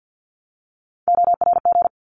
Morse code, heard as a keyed tone, about 50 wpm, 700 Hz. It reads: ORG
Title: ARRL Sections 50wpm